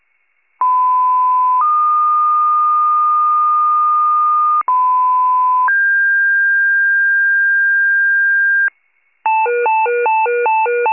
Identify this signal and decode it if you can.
Two-tone page, heard as a one-second tone, then a three-second tone, about 1010 then 1220 Hz, about 1010 then 1600 Hz, with noise. It is heard from 0.6 to 8.7 s.